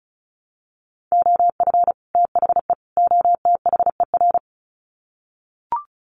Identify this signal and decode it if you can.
Morse code, heard as a keyed tone, about 34 wpm, 700 Hz. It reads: OF THE OTHER
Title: Sets of 3 Words 35wpm